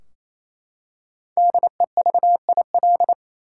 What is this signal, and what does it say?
Morse code, heard as a keyed tone, about 28 wpm, 700 Hz. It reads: DEVIL
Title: Five Letter Words - 28wpm